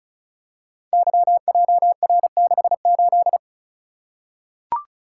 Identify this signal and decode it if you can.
Morse code, heard as a keyed tone, about 36 wpm, 700 Hz. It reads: YJR68
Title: Code Groups - 5 Letter:Numbers - 35wpm